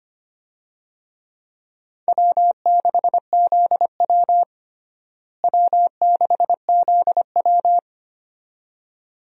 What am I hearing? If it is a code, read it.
Morse code, heard as a keyed tone, about 25 wpm, 700 Hz. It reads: W6ZW W6ZW
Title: USA Call Signs - 4 Characters - Repeated 2x - 25wpm